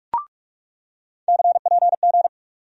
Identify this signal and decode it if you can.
Morse code, heard as a keyed tone, about 41 wpm, 700 Hz. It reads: CPG
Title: USA Call Sign Postfixes - 3 Letters 45wpm